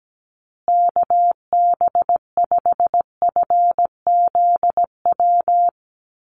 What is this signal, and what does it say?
Morse code, heard as a keyed tone, about 17 wpm, 700 Hz. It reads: KB5FZW
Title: USA Call Signs Repeated 3x 17wpm